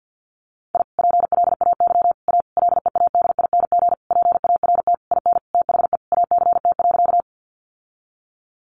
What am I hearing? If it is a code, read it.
Morse code, heard as a keyed tone, about 49 wpm, 700 Hz. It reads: I PLAY A LEADING PART IN THE ACT.
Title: Sentences from Top 1100 Words 50wpm